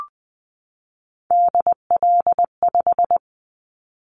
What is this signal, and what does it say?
Morse code, heard as a keyed tone, about 20 wpm, 700 Hz. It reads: DL5
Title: Callsign Prefixes for Top 50 DXCC - 20wpm